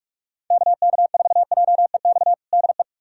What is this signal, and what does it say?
Morse code, heard as a keyed tone, about 41 wpm, 700 Hz. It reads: KK4JEX DE
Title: USA Callsign DE Callsign - 6 Characters - 45wpm